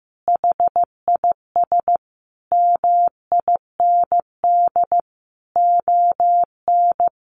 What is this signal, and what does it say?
Morse code, heard as a keyed tone, about 15 wpm, 700 Hz. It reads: HIS MIND ON
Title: Sentences from Top 1400 Words 15wpm